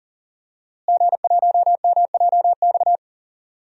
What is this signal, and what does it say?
Morse code, heard as a keyed tone, about 39 wpm, 700 Hz. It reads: G1MJX
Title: DXCC-100 Call Signs - 5 Characters - Repeated 3x - 40wpm